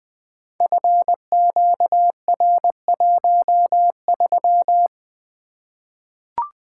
Morse code, heard as a keyed tone, about 20 wpm, 700 Hz. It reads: FQR13
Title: Code Groups - 5 Letter:Numbers - 20wpm